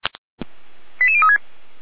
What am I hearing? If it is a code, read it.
Five-tone selective call (ZVEI), heard as quick runs of five tones, tones of about 70 ms. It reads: 88036
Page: SDR/ZVEI